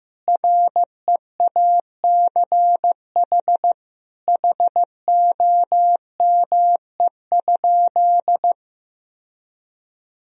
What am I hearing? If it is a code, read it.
Morse code, heard as a keyed tone, about 15 wpm, 700 Hz. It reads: REACH HOME?
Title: Sentences from Top 1000 Words - Review Part 2 - 15wpm